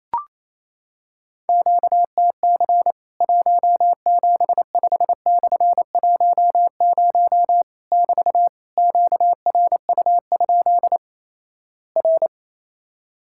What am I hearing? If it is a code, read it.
Morse code, heard as a keyed tone, about 28 wpm, 700 Hz. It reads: QTC 175/10 <BT> QRU? R
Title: Worked All Europe DX Contest - QTC Exchange - 28wpm